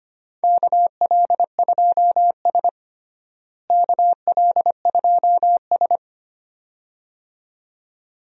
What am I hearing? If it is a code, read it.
Morse code, heard as a keyed tone, about 25 wpm, 700 Hz. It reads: KL2S KL2S
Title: USA Call Signs - 4 Characters - Repeated 2x - 25wpm